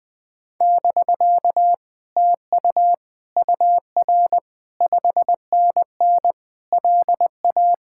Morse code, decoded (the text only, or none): <BK> TU UR 5NN LA